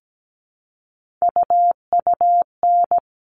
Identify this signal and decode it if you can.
Morse code, heard as a keyed tone, about 17 wpm, 700 Hz. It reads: UUN
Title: All Cut Numbers - 3-Digit 17wpm